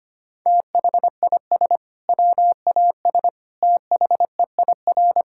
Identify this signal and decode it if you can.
Morse code, heard as a keyed tone, about 25 wpm, 700 Hz. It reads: THIS WAS THEIR
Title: Sentences from Top 1600 Words 25wpm